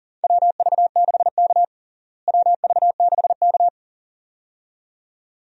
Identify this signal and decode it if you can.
Morse code, heard as a keyed tone, about 40 wpm, 700 Hz. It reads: WV6K WV6K
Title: USA Call Signs - 4 Characters - Repeated 2x - 40wpm